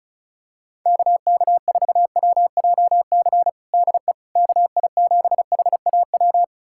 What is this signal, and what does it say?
Morse code, heard as a keyed tone, about 34 wpm, 700 Hz. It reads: KK4WJC DE KI7HAW
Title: USA Callsign DE Callsign - 6 Characters - 35wpm